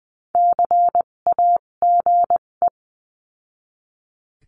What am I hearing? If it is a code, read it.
Morse code, heard as a keyed tone, about 20 wpm, spaced as for 17 wpm, 700 Hz. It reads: CAGE
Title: 257 Lesson 42b - Words 14-20wpm